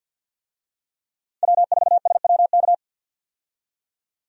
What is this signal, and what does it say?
Morse code, heard as a keyed tone, about 50 wpm, 700 Hz. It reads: W4SWK
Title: USA Call Signs - 5 Characters - 50wpm